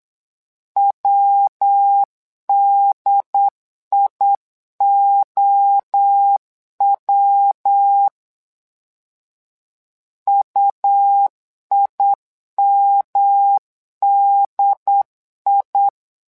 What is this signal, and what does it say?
Morse code, heard as a keyed tone, about 8 wpm, 800 Hz. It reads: WDIOW UIMDI
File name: LEZIONI / 3 MWRO / 3 MWRO 8wpm 2606